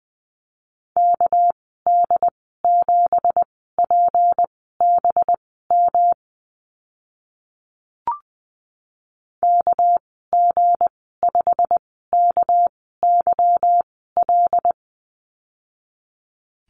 Morse code, heard as a keyed tone, about 20 wpm, spaced as for 15 wpm, 700 Hz. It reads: KD7PBM KG5KYL
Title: 200 Lesson 32g - Letter k - Call Signs 20wpm